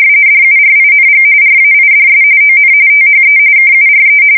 RTTY, about 45.45 baud. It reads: DURING RAIDS ON THE KURE